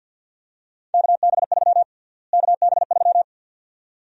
Morse code, heard as a keyed tone, about 49 wpm, 700 Hz. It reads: KB3 KB3
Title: USA Call Sign Prefixes - Repeated 2x - 50wpm